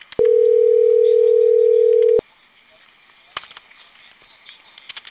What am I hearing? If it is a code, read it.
Ringback tone (North America), 440 + 480 Hz, with speech.
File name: CENTRAL CAEA / out agcaea02 - .16658